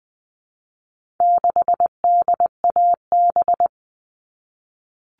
Morse code, heard as a keyed tone, about 20 wpm, 700 Hz. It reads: 6DAB
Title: All Cut Numbers - 4-Digit 20wpm